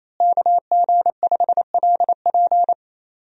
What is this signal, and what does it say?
Morse code, heard as a keyed tone, about 28 wpm, 700 Hz. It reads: KG5LP
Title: USA Call Signs - 5 Characters - Repeated 3x - 28wpm